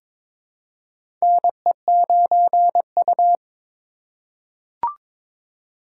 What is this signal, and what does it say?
Morse code, heard as a keyed tone, about 22 wpm, 700 Hz. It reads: NE9U
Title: K1USN SST QSO Practice - 22wpm